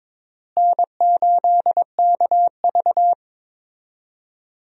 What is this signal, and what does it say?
Morse code, heard as a keyed tone, about 22 wpm, 700 Hz. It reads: N8KV
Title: USA Call Signs - 4 Characters - Repeated 3x - 22wpm